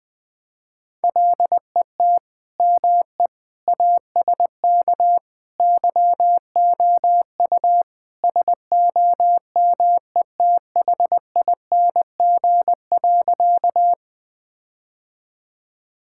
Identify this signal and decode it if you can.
Morse code, heard as a keyed tone, about 20 wpm, 700 Hz. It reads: LET ME ASK YOU SOMETHING.
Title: Sentences with 4-Word Phrases - Spoken Beginning - 20wpm